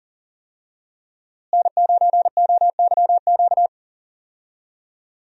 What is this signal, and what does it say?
Morse code, heard as a keyed tone, about 40 wpm, 690 Hz. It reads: N9OYQ
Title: USA Call Signs - 5 Characters - Repeated 3x - 40wpm